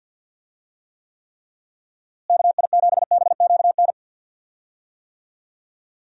Morse code, heard as a keyed tone, about 51 wpm, 700 Hz. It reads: KI7BQN
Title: USA Call Signs - 6 Characters - Repeated 3x - 50wpm